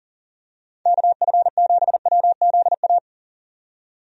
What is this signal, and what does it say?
Morse code, heard as a keyed tone, about 40 wpm, 700 Hz. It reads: KF7WZA
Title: USA Call Signs - 6 Characters - 40wpm